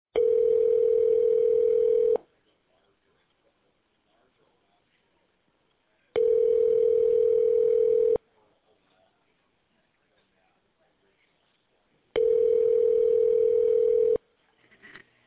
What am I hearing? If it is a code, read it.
Ringback tone (North America), 440 + 480 Hz.